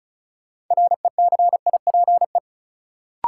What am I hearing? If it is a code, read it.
Morse code, heard as a keyed tone, about 36 wpm, 700 Hz. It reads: RECIPE
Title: Sentences from Top 1200 Words - New Word Drill Alt - 35wpm